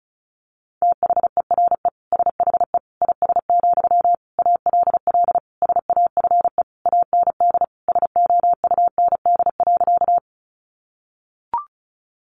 Morse code, heard as a keyed tone, about 34 wpm, 700 Hz. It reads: THERE SHE IS, ALL SAFE AND SOUND.
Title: Sentences from Top 900 Words 35wpm